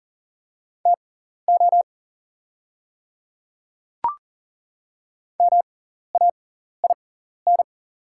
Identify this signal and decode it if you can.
Morse code, heard as a keyed tone, about 40 wpm, spaced as for 14 wpm, 700 Hz. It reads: TO MAIN
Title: 6x Character Spacing - Top 300 Words - 40wpm